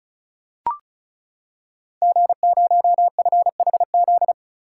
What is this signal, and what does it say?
Morse code, heard as a keyed tone, about 36 wpm, 700 Hz. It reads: G0FHZ
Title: Code Groups - 5 Letter:Numbers - 35wpm